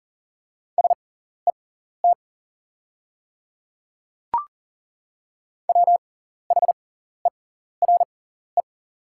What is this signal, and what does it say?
Morse code, heard as a keyed tone, about 40 wpm, spaced as for 14 wpm, 700 Hz. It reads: SET WHERE
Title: 6x Character Spacing - Top 200 Words - 40wpm